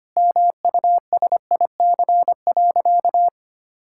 Morse code, heard as a keyed tone, about 25 wpm, 700 Hz. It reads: MUSIC.
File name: practice / 25wpm / Sentences with 3-Word Phrases - 25wpm